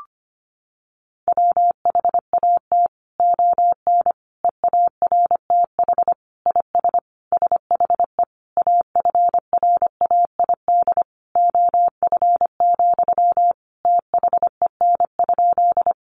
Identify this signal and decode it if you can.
Morse code, heard as a keyed tone, about 25 wpm, 700 Hz. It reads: WHAT ON EARTH IS SHE AFRAID OF, THEN?
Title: Sentences from Top 1500 Words 25wpm